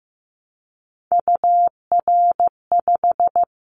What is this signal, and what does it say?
Morse code, heard as a keyed tone, about 15 wpm, 700 Hz. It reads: UR5
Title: Callsign Prefixes for Top 25 DXCC - Repeated 3x - 15wpm